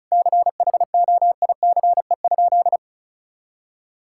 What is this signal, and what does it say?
Morse code, heard as a keyed tone, about 36 wpm, 700 Hz. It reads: CHOICE?
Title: Sentences from Top 1000 Words - Review Part 6 - 35wpm